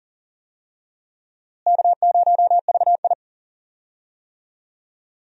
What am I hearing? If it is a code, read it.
Morse code, heard as a keyed tone, about 40 wpm, 700 Hz. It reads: K0VI
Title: USA Call Signs - 4 Characters - Repeated 3x - 40wpm